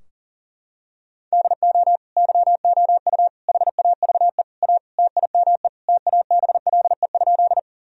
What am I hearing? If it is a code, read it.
Morse code, heard as a keyed tone, about 40 wpm, 700 Hz. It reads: DO YOU HAVE A TIME TABLE?
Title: Sentences from Top 1000 Words - Review Part 2 - 40wpm